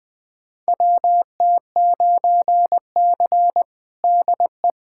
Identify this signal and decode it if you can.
Morse code, heard as a keyed tone, about 21 wpm, 700 Hz. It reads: WT9C DE
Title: USA Callsign DE Callsign - 4 Characters - 20wpm